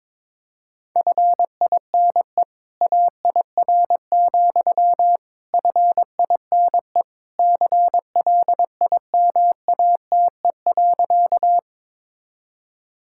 Morse code, heard as a keyed tone, about 22 wpm, 700 Hz. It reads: FINE AIR, FINE CLIMATE.
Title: Sentences from Top 1600 Words 22wpm